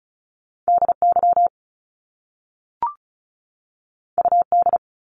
Morse code, heard as a keyed tone, about 36 wpm, 700 Hz. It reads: DY UD
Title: Top Two Letter Combinations in Words 35wpm